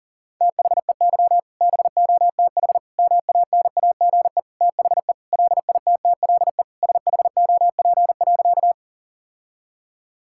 Morse code, heard as a keyed tone, about 40 wpm, 700 Hz. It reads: THEY BOTH MANAGE THE LITTLE SHOP.
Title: Sentences from Top 1600 Words 40wpm